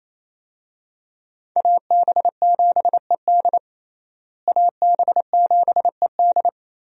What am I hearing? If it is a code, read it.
Morse code, heard as a keyed tone, about 28 wpm, 700 Hz. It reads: AB7ED AB7ED
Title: USA Call Signs - 5 Characters - Repeated 3x - 28wpm